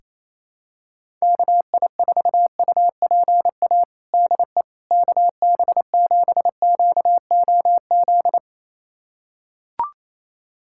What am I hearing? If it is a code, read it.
Morse code, heard as a keyed tone, about 28 wpm, 700 Hz. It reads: KI4UPA DE KB7QOZ
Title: USA Callsign DE Callsign 28wpm